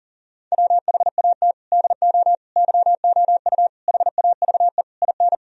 Morse code, heard as a keyed tone, about 40 wpm, 700 Hz. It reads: WHAT DO YOU HAVE IN
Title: Sentences from Top 1000 Words - Review Part 3 - 40wpm